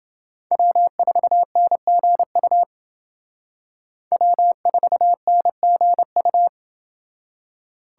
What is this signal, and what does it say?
Morse code, heard as a keyed tone, about 28 wpm, 700 Hz. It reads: W4NGU W4NGU